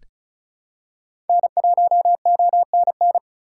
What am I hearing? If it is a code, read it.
Morse code, heard as a keyed tone, about 34 wpm, 700 Hz. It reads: N1ONN
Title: USA Call Signs - 5 Characters - 3x Character Spacing - 35wpm